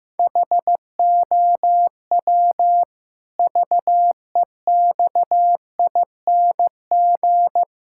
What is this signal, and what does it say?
Morse code, heard as a keyed tone, about 15 wpm, 700 Hz. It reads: HOW VEXING
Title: Pangram Challenge v2 - 15wpm